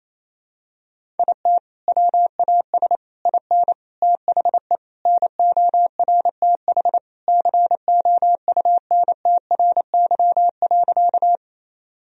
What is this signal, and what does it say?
Morse code, heard as a keyed tone, about 28 wpm, 700 Hz. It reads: IT WAS IN THE NORTH COUNTRY.
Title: Sentences from Top 1000 Words - Review Part 1 - 28wpm